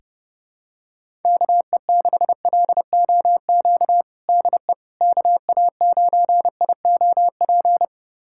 Morse code, heard as a keyed tone, about 30 wpm, 700 Hz. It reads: KE6LOQ DE KA9IOP
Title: USA Callsign DE Callsign 30wpm